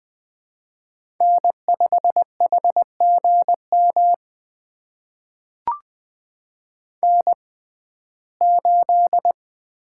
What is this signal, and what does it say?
Morse code, heard as a keyed tone, about 20 wpm, spaced as for 15 wpm, 700 Hz. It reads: N5HGM N8
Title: USA Call Signs - 5 Characters - 6x Character Spacing - 20wpm